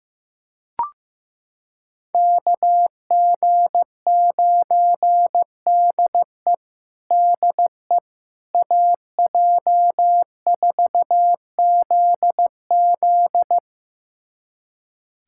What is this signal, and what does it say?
Morse code, heard as a keyed tone, about 15 wpm, 700 Hz. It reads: KG9DE DE AJ4ZZ
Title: USA Callsign DE Callsign - 5 Characters - 15wpm